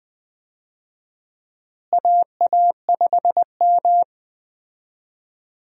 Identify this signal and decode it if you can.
Morse code, heard as a keyed tone, about 20 wpm, 700 Hz. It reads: AA5M
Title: USA Call Signs - 4 Characters - Repeated 2x - 20wpm